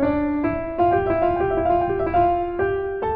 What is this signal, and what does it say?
There is no code.